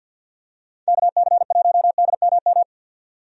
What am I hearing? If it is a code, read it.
Morse code, heard as a keyed tone, about 51 wpm, 700 Hz. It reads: KC1DMM
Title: USA Call Signs - 6 Characters - 50wpm